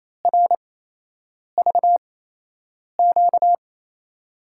Morse code, heard as a keyed tone, about 28 wpm, spaced as for 8 wpm, 700 Hz. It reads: RVQ
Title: USA Call Signs - 6 Characters - 8x Character Spacing - 28wpm